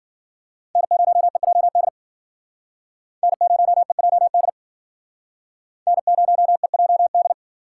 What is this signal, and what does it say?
Morse code, heard as a keyed tone, about 51 wpm, 700 Hz. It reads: N0EJD N0EJD N0EJD